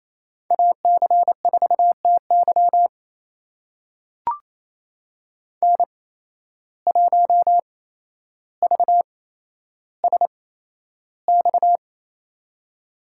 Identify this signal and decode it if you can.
Morse code, heard as a keyed tone, about 28 wpm, spaced as for 12 wpm, 700 Hz. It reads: AC4TY N1VSX
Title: USA Call Signs - 5 Characters - 8x Character Spacing - 28wpm